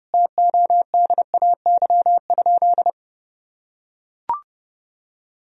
Morse code, heard as a keyed tone, about 30 wpm, 700 Hz. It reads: TODAY?
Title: Sentences from Top 1000 Words - Review Part 4 - 30wpm